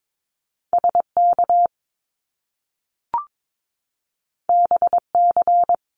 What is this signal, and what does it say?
Morse code, heard as a keyed tone, about 22 wpm, 700 Hz. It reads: SK BC
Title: Canadian Provinces and Territories 22wpm